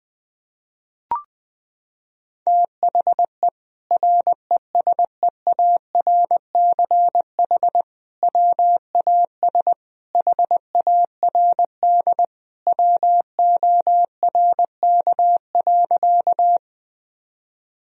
Morse code, heard as a keyed tone, about 20 wpm, 700 Hz. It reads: THE RESEARCH WAS HARD WORK.
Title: Sentences from Top 1000 Words - Review Part 4 - 20wpm